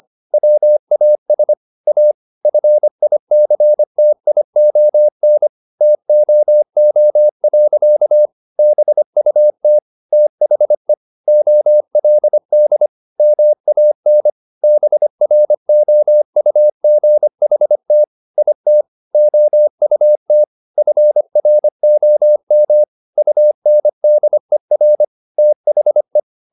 Morse code, heard as a keyed tone, about 25 wpm, 600 Hz. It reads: WAS A FICTION TOO. BUT THE OLD MAN BROUGHT IT OUT FROM UNDER THE
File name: the old man and the sea part 1 25wpm